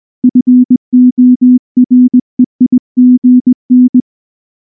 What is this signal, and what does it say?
Morse code, heard as a keyed tone, about 19 wpm, 250 Hz. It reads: FOREIGN
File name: morse / morse-training-20WPM-2